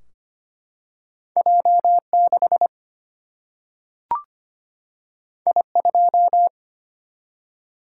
Morse code, heard as a keyed tone, about 25 wpm, 700 Hz. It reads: J6 I2
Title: Single-Letter Single-Number 25wpm